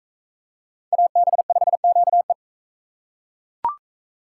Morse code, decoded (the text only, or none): AB5QE